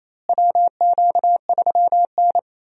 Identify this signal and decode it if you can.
Morse code, heard as a keyed tone, about 28 wpm, 700 Hz. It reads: WQ3N
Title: USA Call Signs - 4 Characters - Repeated 3x - 28wpm